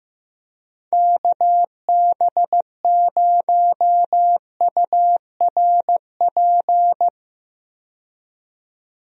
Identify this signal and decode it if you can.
Morse code, heard as a keyed tone, about 16 wpm, 700 Hz. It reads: KB0URP
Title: USA Call Signs - 6 Characters - 15wpm